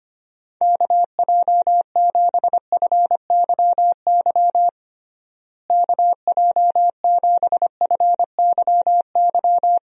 Morse code, heard as a keyed tone, about 25 wpm, 700 Hz. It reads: KJ7FYY KJ7FYY
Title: USA Call Signs - 6 Characters - Repeated 3x - 25wpm